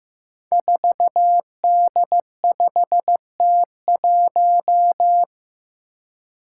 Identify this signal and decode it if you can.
Morse code, heard as a keyed tone, about 15 wpm, 700 Hz. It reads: 4D5T1
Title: Five Challenging Letter-Number Combinations 15wpm